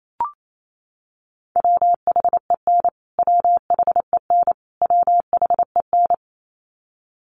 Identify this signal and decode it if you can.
Morse code, heard as a keyed tone, about 28 wpm, 700 Hz. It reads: WHEN WHEN WHEN
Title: Top 500 Words Repeated 28wpm